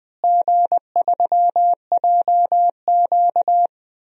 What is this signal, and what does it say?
Morse code, heard as a keyed tone, about 20 wpm, 700 Hz. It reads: G3JQ
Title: DXCC-100 Call Signs - 4 Characters - Repeated 3x - 20wpm